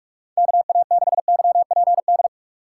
Morse code, heard as a keyed tone, about 46 wpm, 700 Hz. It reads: KA6YPD
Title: USA Call Signs - 3x Character Spacing - 45wpm